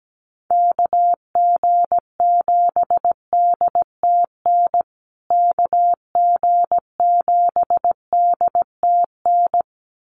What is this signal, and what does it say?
Morse code, heard as a keyed tone, about 17 wpm, 700 Hz. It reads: KG7DTN KG7DTN
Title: USA Call Signs Repeated 2x 17wpm